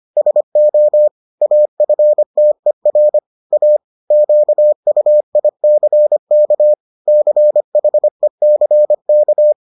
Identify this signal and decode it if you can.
Morse code, heard as a keyed tone, about 25 wpm, 600 Hz. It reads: SO AFTER A QUICK CHECK